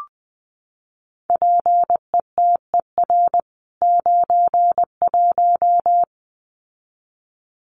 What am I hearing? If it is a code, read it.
Morse code, heard as a keyed tone, about 20 wpm, 700 Hz. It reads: PETER 91
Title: CWT Exchange 20wpm